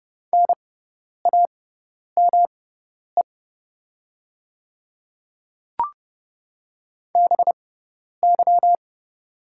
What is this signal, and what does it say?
Morse code, heard as a keyed tone, about 30 wpm, spaced as for 10 wpm, 690 Hz. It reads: NAME BY
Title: 6x Character Spacing - Top 200 Words - 30wpm